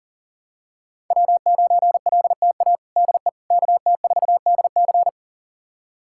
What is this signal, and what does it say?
Morse code, heard as a keyed tone, about 37 wpm, 700 Hz. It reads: W9LTA DE KT4DC
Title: USA Callsign DE Callsign - 5 Characters - 40wpm